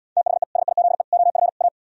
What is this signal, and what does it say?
Morse code, heard as a keyed tone, about 77 wpm, 700 Hz. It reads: THE SIZE OF A